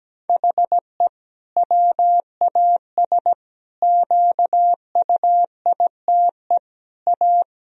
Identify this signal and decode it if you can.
Morse code, heard as a keyed tone, about 17 wpm, 700 Hz. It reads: HE WAS QUITE A
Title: Sentences from Top 1000 Words - Review Part 3 - 17wpm